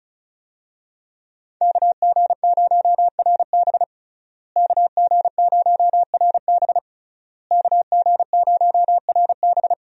Morse code, heard as a keyed tone, about 36 wpm, 700 Hz. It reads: KG0RB KG0RB KG0RB
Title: USA Call Signs - 5 Characters - Repeated 3x - 35wpm